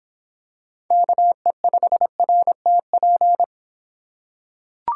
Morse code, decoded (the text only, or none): KE5RTP